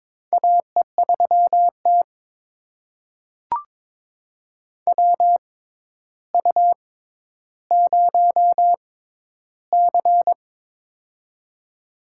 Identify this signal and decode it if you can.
Morse code, heard as a keyed tone, about 23 wpm, spaced as for 11 wpm, 700 Hz. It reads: AE3T WU0C
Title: USA Call Signs - 4 Characters - 6x Character Spacing - 22wpm